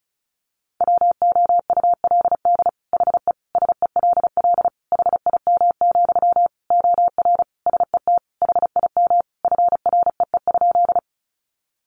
Morse code, heard as a keyed tone, about 34 wpm, 700 Hz. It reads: WOULD HE SELL HIM, OR SET HIM FREE?
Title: Sentences from Top 1100 Words 35wpm